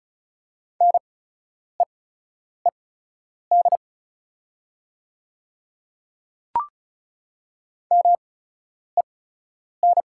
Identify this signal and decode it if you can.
Morse code, heard as a keyed tone, about 34 wpm, spaced as for 10 wpm, 690 Hz. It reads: NEED MEN
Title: 8x Character Spacing - Top 500 Words - 35wpm